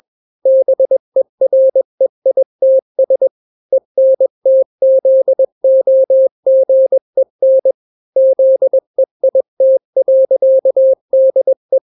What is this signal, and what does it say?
Morse code, heard as a keyed tone, about 21 wpm, 530 Hz. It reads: BEREITS ENTZOGEN ZEIT.DE
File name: daily / 20wpm